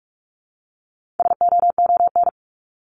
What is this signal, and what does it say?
Morse code, heard as a keyed tone, about 44 wpm, 700 Hz. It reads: SOON
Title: Top 500 Words 45wpm